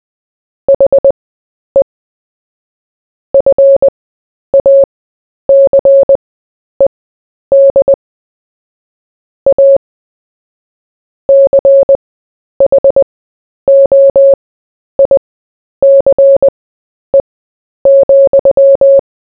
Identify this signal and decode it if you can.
Morse code, heard as a keyed tone, about 20 wpm, spaced as for 10 wpm, 550 Hz. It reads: HE FACED A CHOICE,
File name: cw / fists / Short-Story-4-10wpm